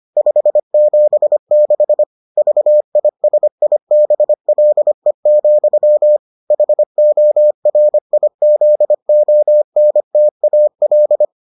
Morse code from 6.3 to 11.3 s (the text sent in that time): HORIZONTAL